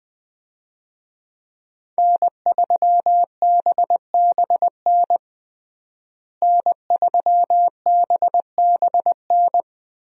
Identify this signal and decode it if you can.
Morse code, heard as a keyed tone, about 20 wpm, 700 Hz. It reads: N3BBN N3BBN
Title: USA Call Signs - 5 Characters - Repeated 2x - 20wpm